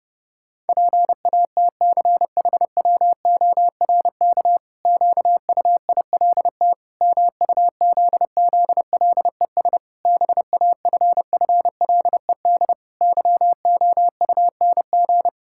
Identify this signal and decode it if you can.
Morse code, heard as a keyed tone, about 30 wpm, 700 Hz. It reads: PATCHWORK QUILT MUZZLES BAFFLED YOUNG
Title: Pangram Challenge v2 - 30wpm